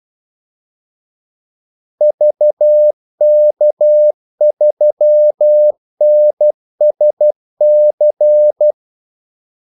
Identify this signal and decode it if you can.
Morse code, heard as a keyed tone, about 12 wpm, 600 Hz. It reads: VK3NSC
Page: 2 ekstra ordmellemrum mellem kaldesignalerne